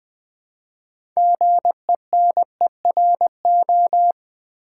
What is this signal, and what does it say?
Morse code, heard as a keyed tone, about 20 wpm, 700 Hz. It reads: GENERO
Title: Build A Word - 2 Letters At A Time - 8-Letter Words - 20wpm